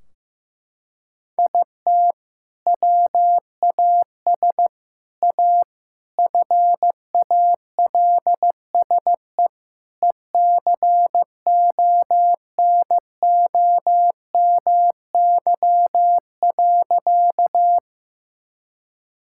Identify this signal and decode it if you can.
Morse code, heard as a keyed tone, about 15 wpm, 700 Hz. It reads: IT WAS A FALSE ECONOMY.
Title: Sentences from Top 1600 Words 15wpm